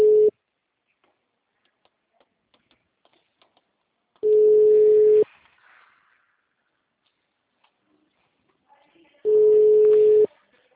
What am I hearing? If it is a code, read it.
Ringback tone (Europe, 425 Hz).